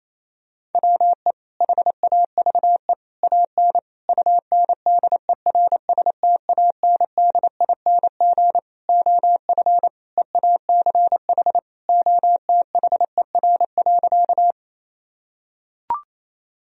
Morse code, heard as a keyed tone, about 28 wpm, 700 Hz. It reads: WE HAVE AN UNDERSTANDING OF EACH OTHER.
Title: Sentences from Top 1000 Words - Review Part 3 - 28wpm